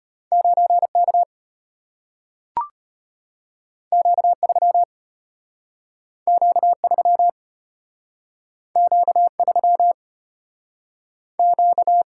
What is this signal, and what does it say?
Morse code, heard as a keyed tone, about 34 wpm, 700 Hz. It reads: 9K Q3 Q3 Q3 Q